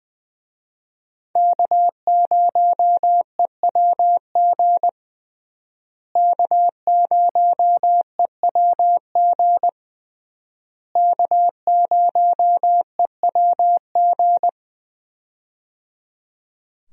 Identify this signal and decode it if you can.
Morse code, heard as a keyed tone, about 20 wpm, 700 Hz. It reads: K0EWG K0EWG K0EWG
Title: USA Call Signs - 5 Characters - Repeated 3x - 20wpm